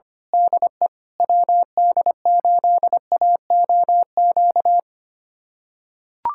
Morse code, heard as a keyed tone, about 25 wpm, 700 Hz. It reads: DE WD8AOQ
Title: USA Callsign DE Callsign - 6 Characters - 25wpm